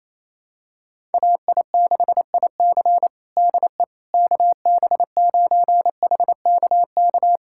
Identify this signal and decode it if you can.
Morse code, heard as a keyed tone, about 28 wpm, 700 Hz. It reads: AI6IC DE KB9HKK
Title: USA Callsign DE Callsign 28wpm